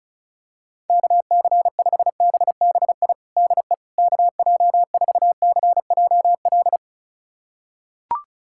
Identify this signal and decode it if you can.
Morse code, heard as a keyed tone, about 36 wpm, 700 Hz. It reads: KC5BBI DE KJ4CJL
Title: USA Callsign DE Callsign - 6 Characters - 35wpm